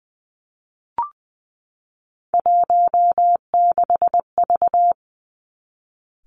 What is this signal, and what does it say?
Morse code, heard as a keyed tone, about 20 wpm, 700 Hz. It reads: 16V
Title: Three Challenging Letter-Number Combinations 20wpm